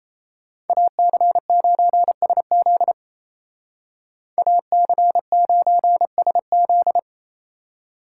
Morse code, heard as a keyed tone, about 35 wpm, 700 Hz. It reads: AC9SZ AC9SZ